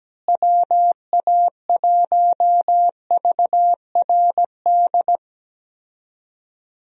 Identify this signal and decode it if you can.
Morse code, heard as a keyed tone, about 17 wpm, 700 Hz. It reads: WA1VRD
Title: USA Call Signs - 6 Characters - Repeated 3x - 17wpm